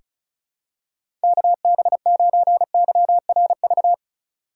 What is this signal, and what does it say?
Morse code, heard as a keyed tone, about 36 wpm, 700 Hz. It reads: KB9YRV
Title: USA Call Signs - 6 Characters - 3x Character Spacing - 35wpm